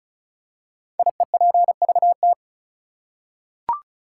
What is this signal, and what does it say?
Morse code, heard as a keyed tone, about 36 wpm, 700 Hz. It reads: IEPVT
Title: Code Groups - 5 Letters - 35wpm